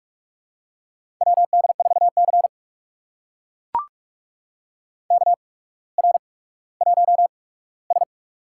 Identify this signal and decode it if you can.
Morse code, heard as a keyed tone, about 44 wpm, spaced as for 19 wpm, 690 Hz. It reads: WD4C KR1S
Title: USA Call Signs - 4 Characters - 8x Character Spacing - 45wpm